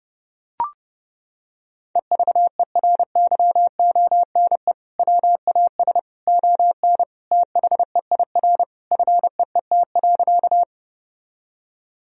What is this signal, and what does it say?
Morse code, heard as a keyed tone, about 32 wpm, 700 Hz. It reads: EVERYONE WAS ON THEIR FEET.
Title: Sentences from Top 1000 Words - Review Part 1 - 30wpm